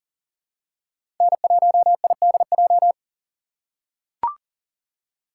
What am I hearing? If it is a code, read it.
Morse code, heard as a keyed tone, about 40 wpm, 700 Hz. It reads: N1IDJ
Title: USA Call Signs - 5 Characters - 4x Character Spacing - 40wpm